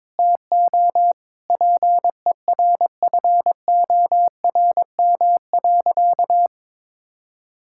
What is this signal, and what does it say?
Morse code, heard as a keyed tone, about 22 wpm, 700 Hz. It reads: TO PERFORM.
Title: Sentences from Top 1000 Words - Review Part 5 - 22wpm